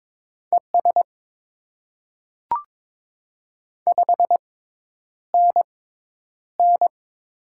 Morse code, heard as a keyed tone, about 22 wpm, spaced as for 10 wpm, 700 Hz. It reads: ES 5NN
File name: Top 20 QSO Elements - 6x Character Spacing - 22wpm